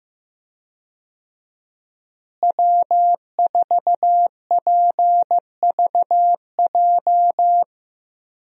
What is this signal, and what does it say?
Morse code, heard as a keyed tone, about 15 wpm, 700 Hz. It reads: W4PVJ
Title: USA Call Signs - 5 Characters - Repeated 2x - 15wpm